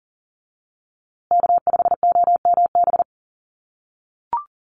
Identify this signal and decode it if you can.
Morse code, heard as a keyed tone, about 40 wpm, 700 Hz. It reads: K5OMB
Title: USA Call Signs Repeated 2x 40wpm